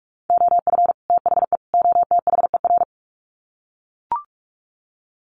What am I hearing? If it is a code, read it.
Morse code, heard as a keyed tone, about 44 wpm, 700 Hz. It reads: OF THE OTHER
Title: Sets of 3 Words 45wpm